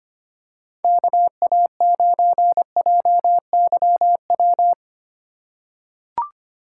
Morse code, heard as a keyed tone, about 25 wpm, 700 Hz. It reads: KA9JYW
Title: USA Call Signs - 6 Characters - 3x Character Spacing - 25wpm